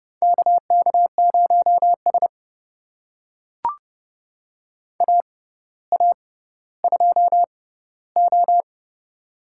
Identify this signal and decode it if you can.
Morse code, heard as a keyed tone, about 30 wpm, spaced as for 15 wpm, 700 Hz. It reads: KK0S AA2O
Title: USA Call Signs - 4 Characters - 6x Character Spacing - 30wpm